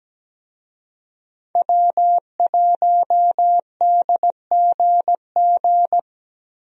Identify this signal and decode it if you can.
Morse code, heard as a keyed tone, about 17 wpm, 700 Hz. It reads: W1DGG
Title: USA Call Signs - 5 Characters - Repeated 3x - 17wpm